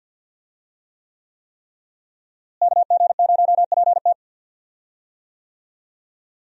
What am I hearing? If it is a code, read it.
Morse code, heard as a keyed tone, about 52 wpm, 700 Hz. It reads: KG0PT
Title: USA Call Signs - 5 Characters - Repeated 3x - 50wpm